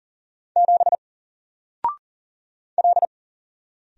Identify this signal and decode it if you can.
Morse code, heard as a keyed tone, about 40 wpm, 700 Hz. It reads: 7 L
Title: Single Character (Letter:Number:Punctuation:Prosign) Alt 40wpm